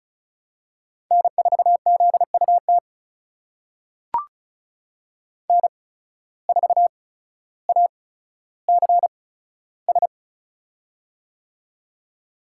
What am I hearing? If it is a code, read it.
Morse code, heard as a keyed tone, about 36 wpm, spaced as for 15 wpm, 700 Hz. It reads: N4ZUT N4ACS
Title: USA Call Signs - 5 Characters - 8x Character Spacing - 35wpm